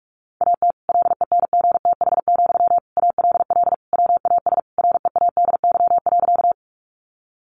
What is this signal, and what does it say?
Morse code, heard as a keyed tone, about 44 wpm, 700 Hz. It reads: AT LENGTH, ALL WAS READY.
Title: Sentences from Top 1000 Words 45wpm